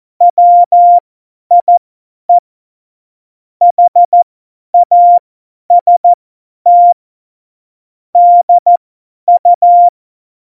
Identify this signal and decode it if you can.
Morse code, heard as a keyed tone, about 14 wpm, spaced as for 10 wpm, 700 Hz. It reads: WIE HAST DU
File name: Morsekurs / Witze / Witz 22. Dezember 2020, 50 F70